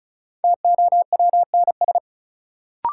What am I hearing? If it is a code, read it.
Morse code, heard as a keyed tone, about 36 wpm, 700 Hz. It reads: TOWNS
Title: Sentences from Top 1000 Words - Review Part 6 - New Word Drill - 35wpm